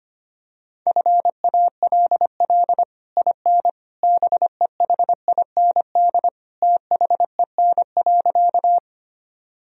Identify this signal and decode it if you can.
Morse code, heard as a keyed tone, about 25 wpm, 700 Hz. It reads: FALL IN BEHIND THEN.
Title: Sentences from Top 1000 Words - Review Part 2 - 25wpm